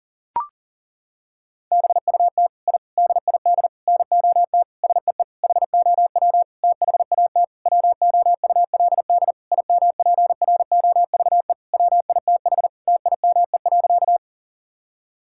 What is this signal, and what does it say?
Morse code, heard as a keyed tone, about 40 wpm, 700 Hz. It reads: BUT I DID NOT SEE HOW THAT WOULD IMPROVE WITH TIME.
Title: Sentences from Top 1000 Words - Review Part 5 - 40wpm